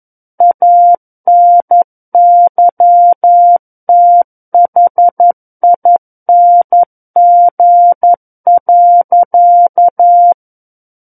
Morse code, heard as a keyed tone, about 11 wpm, 700 Hz. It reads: ANYTHING.
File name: 11 WPM